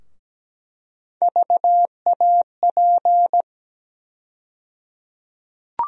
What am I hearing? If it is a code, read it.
Morse code, heard as a keyed tone, about 17 wpm, 700 Hz. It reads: VAP
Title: USA Call Sign Postfixes - 3 Letters 17wpm